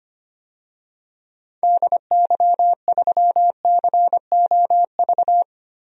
Morse code, heard as a keyed tone, about 25 wpm, 700 Hz. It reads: DY3COV
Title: DXCC-100 Call Signs - 6 Characters - 25wpm